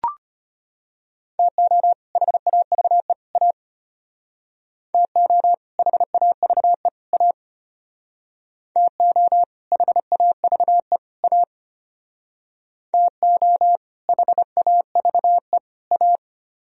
Morse code, heard as a keyed tone, about 33 wpm, 700 Hz. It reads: TO HAVE A TO HAVE A TO HAVE A TO HAVE A